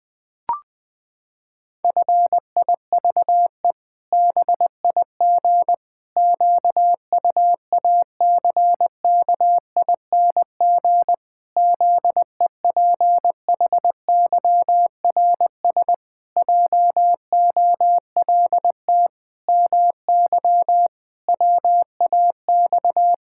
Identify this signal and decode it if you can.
Morse code, heard as a keyed tone, about 21 wpm, 700 Hz. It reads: FIVE BIG QUACKING ZEPHYRS JOLT MY WAX
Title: Pangram Challenge v2 - 20wpm